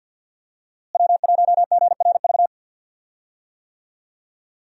Morse code, heard as a keyed tone, about 51 wpm, 700 Hz. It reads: W1GRV
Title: USA Call Signs - 5 Characters - Repeated 2x - 50wpm